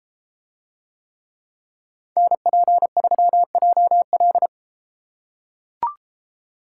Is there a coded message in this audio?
Yes: Morse code from 2.2 to 4.5 s, heard as a keyed tone, about 30 wpm, 700 Hz. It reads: NP3JL